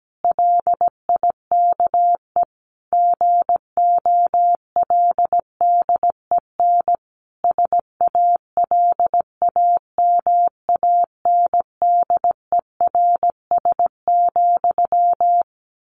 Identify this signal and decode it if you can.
Morse code, heard as a keyed tone, about 17 wpm, 700 Hz. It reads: LIKE GOLDEN SALAMANDERS,
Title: A Sound Of Thunder 17wpm